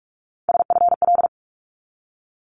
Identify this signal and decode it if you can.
Morse code, heard as a keyed tone, about 46 wpm, 700 Hz. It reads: SFL
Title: ARRL Sections 45wpm